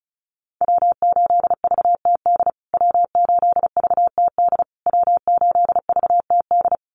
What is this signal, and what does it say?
Morse code, heard as a keyed tone, about 34 wpm, 700 Hz. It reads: W8VTD W8VTD W8VTD
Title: USA Call Signs Repeated 3x 35wpm